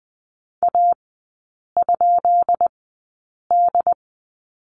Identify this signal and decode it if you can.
Morse code, heard as a keyed tone, about 20 wpm, spaced as for 8 wpm, 700 Hz. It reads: A?D
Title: 133 Lesson 22e - Question Mark - Three Characters 20wpm